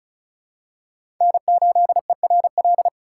Morse code, heard as a keyed tone, about 34 wpm, 700 Hz. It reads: N8ERL
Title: USA Call Signs - 5 Characters - 2x Character Spacing - 35wpm